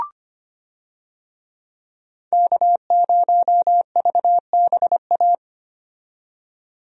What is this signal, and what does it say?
Morse code, heard as a keyed tone, about 25 wpm, 690 Hz. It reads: K0VBA
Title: USA Call Signs - 5 Characters - 25wpm